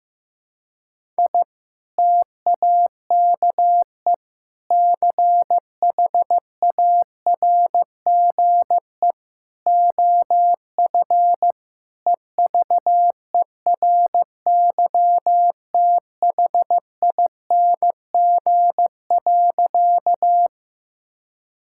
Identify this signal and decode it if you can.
Morse code, heard as a keyed tone, about 15 wpm, 700 Hz. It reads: I TAKE CHARGE OF EVERYTHING.
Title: Sentences from Top 1000 Words - Review Part 6 - 15wpm